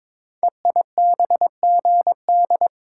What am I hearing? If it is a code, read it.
Morse code, heard as a keyed tone, about 22 wpm, 700 Hz. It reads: EIBGD
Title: Code Groups - 5 Letters - 22wpm